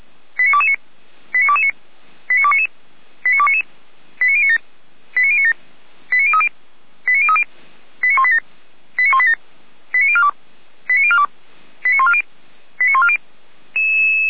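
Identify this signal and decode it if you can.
Five-tone selective call (ZVEI), heard as quick runs of five tones, tones of about 70 ms. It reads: 89229 89229 89299 89299 89097 89097 89030 89030 89187 89187 89042 89042 89140 89140
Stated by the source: Five-tone ZVEI selective call received by SDR, archived by Feuerwehr Orlamünde.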